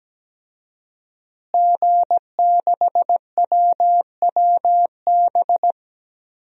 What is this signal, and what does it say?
Morse code, heard as a keyed tone, about 17 wpm, 700 Hz. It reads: G6WWB
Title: DXCC-100 Call Signs - 5 Characters - Repeated 2x - 17wpm